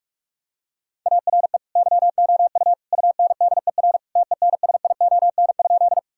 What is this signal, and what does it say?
Morse code, heard as a keyed tone, about 46 wpm, 700 Hz. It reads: ARE YOU UNDER TENSION?
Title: Sentences from Top 1600 Words 45wpm